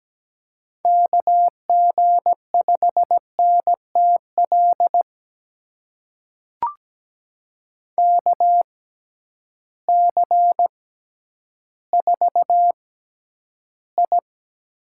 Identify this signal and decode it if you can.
Morse code, heard as a keyed tone, about 17 wpm, spaced as for 10 wpm, 700 Hz. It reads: KG5NTL KC4I
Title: USA Call Signs - 6 Characters - 6x Character Spacing - 17wpm